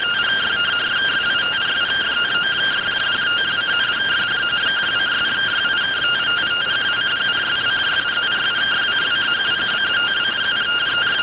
Two-tone teleprinter signal, about 170 Hz shift.